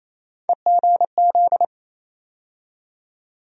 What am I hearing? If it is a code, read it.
Morse code, heard as a keyed tone, about 28 wpm, 700 Hz. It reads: EGZ
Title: USA Call Sign Postfixes - 3 Letters 28wpm